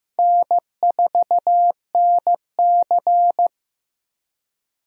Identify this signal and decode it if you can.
Morse code, heard as a keyed tone, about 15 wpm, 700 Hz. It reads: N4NC
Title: USA Call Signs - 4 Characters - Repeated 3x - 15wpm